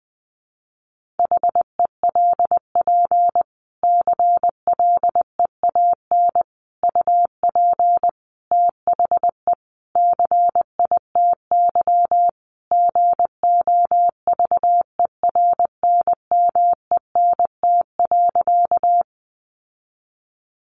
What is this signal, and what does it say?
Morse code, heard as a keyed tone, about 19 wpm, 700 Hz. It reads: HELP CLEAN UP THE CITY GOVERNMENT.
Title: Sentences from Top 1100 Words 20wpm